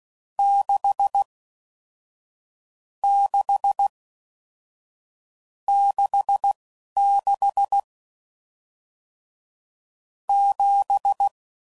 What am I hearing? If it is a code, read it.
Morse code, heard as a keyed tone, about 16 wpm, spaced as for 11 wpm, 800 Hz. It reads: 6 6 66 7